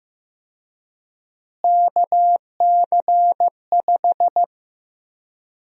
Morse code, heard as a keyed tone, about 15 wpm, 700 Hz. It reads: KC5
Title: USA Call Sign Prefixes - Repeated 2x - 15wpm